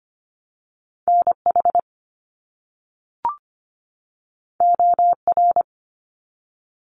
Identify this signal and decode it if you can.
Morse code, heard as a keyed tone, about 25 wpm, 700 Hz. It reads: NH OR
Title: US States 25wpm